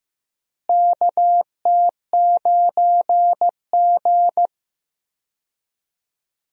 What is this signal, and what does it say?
Morse code, heard as a keyed tone, about 15 wpm, 700 Hz. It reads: KT9G
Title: USA Call Signs - 4 Characters - Repeated 2x - 15wpm